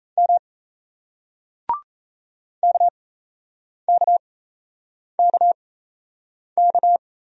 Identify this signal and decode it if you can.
Morse code, heard as a keyed tone, about 35 wpm, spaced as for 8 wpm, 700 Hz. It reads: M KKKK
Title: Speed-Racing Single Letters 28wpm